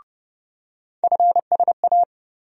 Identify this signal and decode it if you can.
Morse code, heard as a keyed tone, about 29 wpm, 700 Hz. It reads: FSA
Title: USA Call Sign Postfixes - 3 Letters 30wpm